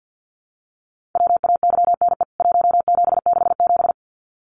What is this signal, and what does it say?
Morse code, heard as a keyed tone, about 50 wpm, 700 Hz. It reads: WAYNE 1767
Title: CWT Exchange 50wpm